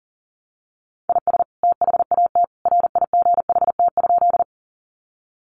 Morse code, heard as a keyed tone, about 40 wpm, 700 Hz. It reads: IS THAT RIGHT?
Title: Sentences from Top 200 Words 40wpm